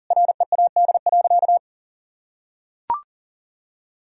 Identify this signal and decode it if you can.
Morse code, heard as a keyed tone, about 40 wpm, 700 Hz. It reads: READ.
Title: Sentences from Top 1000 Words - Review Part 1 - 40wpm